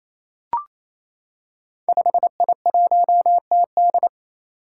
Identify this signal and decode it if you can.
Morse code, heard as a keyed tone, about 27 wpm, 700 Hz. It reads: 5I1TD
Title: Five Challenging Letter-Number Combinations 28wpm